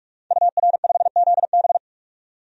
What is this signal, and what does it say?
Morse code, heard as a keyed tone, about 41 wpm, 700 Hz. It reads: UR5ZB
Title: DXCC-100 Call Signs - 5 Characters - 45wpm